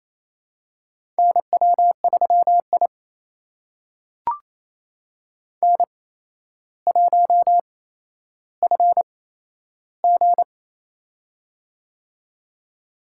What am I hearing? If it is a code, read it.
Morse code, heard as a keyed tone, about 28 wpm, spaced as for 12 wpm, 700 Hz. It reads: NW3I N1FG
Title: USA Call Signs - 4 Characters - 8x Character Spacing - 28wpm